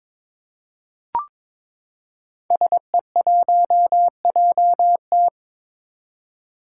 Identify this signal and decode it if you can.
Morse code, heard as a keyed tone, about 22 wpm, 700 Hz. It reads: SE1JT
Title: Five Challenging Letter-Number Combinations 22wpm